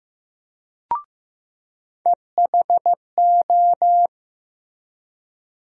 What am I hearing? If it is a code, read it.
Morse code, heard as a keyed tone, about 15 wpm, 700 Hz. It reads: EHO
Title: USA Call Sign Postfixes - 3 Letters 15wpm